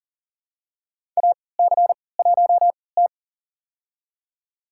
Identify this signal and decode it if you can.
Morse code, heard as a keyed tone, about 37 wpm, spaced as for 22 wpm, 700 Hz. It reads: AC1T
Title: USA Call Signs - 4 Characters - 3x Character Spacing - 40wpm